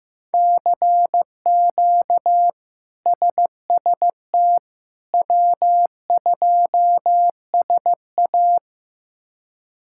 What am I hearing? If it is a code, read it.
Morse code, heard as a keyed tone, about 15 wpm, 700 Hz. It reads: CQ SST W2SA
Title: K1USN SST QSO Practice - 15wpm